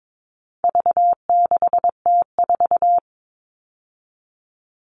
Morse code, heard as a keyed tone, about 22 wpm, 700 Hz. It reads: V6T4
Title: All Cut Numbers - 4-Digit 22wpm